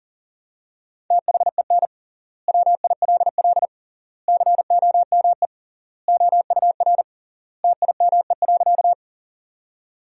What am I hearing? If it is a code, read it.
Morse code, heard as a keyed tone, about 40 wpm, 700 Hz. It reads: THEN WILL COME OUR TIME.
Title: 3x Word-Spacing - Sentences from Top 200 Words - 40wpm